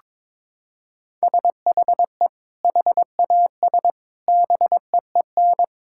Morse code, heard as a keyed tone, about 21 wpm, 700 Hz. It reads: SHE HAS BEEN
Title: Sentences from Top 1000 Words - Review Part 6 - 22wpm